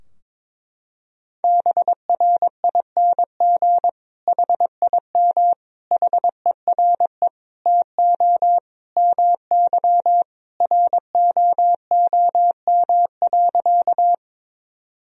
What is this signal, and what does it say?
Morse code, heard as a keyed tone, about 22 wpm, 700 Hz. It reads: BRING HIM HERE TO MY ROOM.
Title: Sentences from Top 1000 Words - Review Part 3 - 22wpm